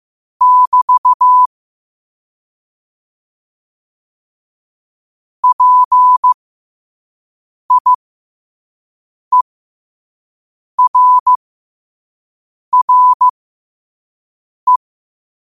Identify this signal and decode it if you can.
Morse code, heard as a keyed tone, about 15 wpm, spaced as for 5 wpm, 1000 Hz. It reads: <BT> PIERRE
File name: qotdfiles / 5wpm